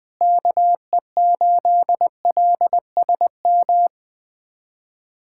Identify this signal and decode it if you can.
Morse code, heard as a keyed tone, about 21 wpm, 700 Hz. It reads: KE8LSM
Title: USA Callsign DE Callsign - 6 Characters - 20wpm